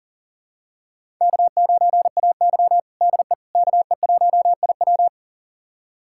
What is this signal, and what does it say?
Morse code, heard as a keyed tone, about 37 wpm, 700 Hz. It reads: K9AY DE KE1IW
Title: USA Callsign DE Callsign 40wpm